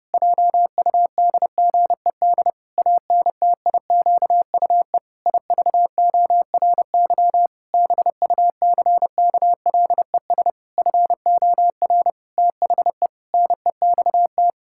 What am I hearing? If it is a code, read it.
Morse code, heard as a keyed tone, about 30 wpm, 700 Hz. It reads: JUDGED ANTIQUE IVORY BUCKLES FOR THE NEXT
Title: Pangram Challenge v2 - 30wpm